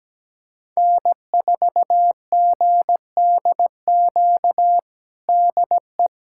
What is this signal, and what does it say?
Morse code, heard as a keyed tone, about 17 wpm, 700 Hz. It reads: N4GDQ DE
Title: USA Callsign DE Callsign - 5 Characters - 17wpm